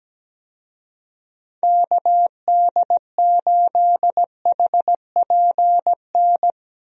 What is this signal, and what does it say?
Morse code, heard as a keyed tone, about 17 wpm, 700 Hz. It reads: KD8HPN
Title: USA Call Signs - 6 Characters - Repeated 3x - 17wpm